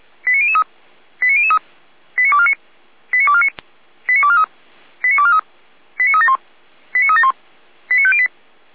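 Five-tone selective call (ZVEI), heard as quick runs of five tones, tones of about 70 ms. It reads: 89003 89003 89249 89249 89243 89243 89371 89371 89508
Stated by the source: ZVEI five-tone selective call 89250 received by SDR, from the Feuerwehr Orlamünde fire brigade audio server.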